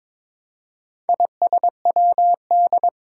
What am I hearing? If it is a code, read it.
Morse code, heard as a keyed tone, about 22 wpm, 700 Hz. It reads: ISWD
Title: Four Challenging Letter-Number Combinations 22wpm